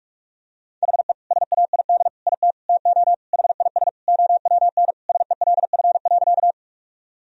Morse code, heard as a keyed tone, about 41 wpm, 700 Hz. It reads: HE SAID IT TO HIS OWN SELF.
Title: Sentences from Top 1000 Words - Review Part 1 - 45wpm